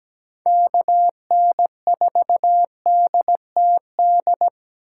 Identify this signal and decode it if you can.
Morse code, heard as a keyed tone, about 22 wpm, 700 Hz. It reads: KN4DTD